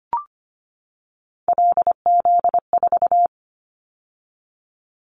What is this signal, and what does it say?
Morse code, heard as a keyed tone, about 25 wpm, 700 Hz. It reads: LZ4
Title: Callsign Prefixes for Top 100 DXCC - 25wpm